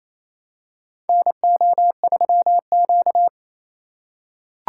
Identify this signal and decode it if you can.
Morse code, heard as a keyed tone, about 28 wpm, 700 Hz. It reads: NO3Q
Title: USA Call Signs - 4 Characters - 4x Character Spacing - 28wpm